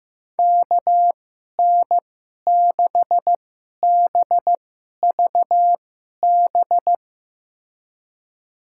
Morse code, heard as a keyed tone, about 15 wpm, spaced as for 11 wpm, 700 Hz. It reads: KN6BVB
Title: USA Call Signs - 6 Characters - 2x Character Spacing - 15wpm